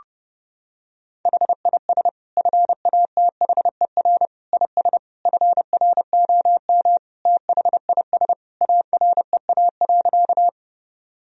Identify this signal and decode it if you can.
Morse code, heard as a keyed tone, about 30 wpm, 700 Hz. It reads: HIS FATHER IS FROM THIS AREA.
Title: Sentences from Top 1000 Words - Review Part 2 - 30wpm